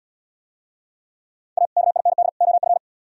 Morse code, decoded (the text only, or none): A KIND OF